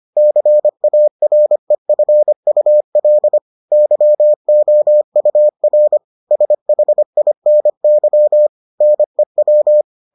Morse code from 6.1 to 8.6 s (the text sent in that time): SHINY